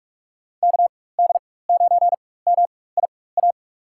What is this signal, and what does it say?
Morse code, heard as a keyed tone, about 41 wpm, spaced as for 20 wpm, 700 Hz. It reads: KD9MIA
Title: USA Call Signs - 4x Character Spacing - 45wpm